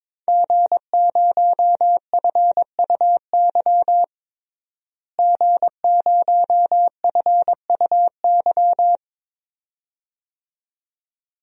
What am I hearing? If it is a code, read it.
Morse code, heard as a keyed tone, about 22 wpm, 700 Hz. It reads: G0FUY G0FUY
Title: DXCC-100 Call Signs - 5 Characters - Repeated 2x - 22wpm